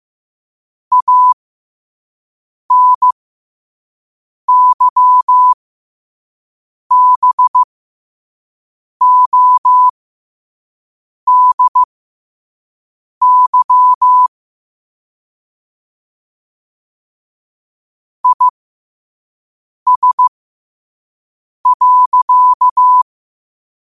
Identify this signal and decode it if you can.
Morse code, heard as a keyed tone, about 15 wpm, spaced as for 5 wpm, 1000 Hz. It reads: ANYBODY IS.
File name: qotdfiles / 5wpm